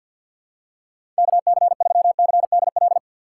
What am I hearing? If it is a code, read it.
Morse code, heard as a keyed tone, about 52 wpm, 700 Hz. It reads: KC3CDL
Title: USA Call Signs - 6 Characters - 4x Character Spacing - 50wpm